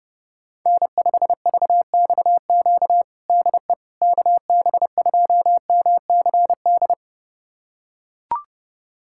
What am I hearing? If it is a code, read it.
Morse code, heard as a keyed tone, about 30 wpm, 700 Hz. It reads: N5VXQ DE KB2MCD
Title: USA Callsign DE Callsign 30wpm